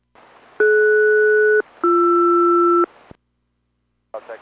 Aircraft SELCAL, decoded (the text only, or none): DS-BR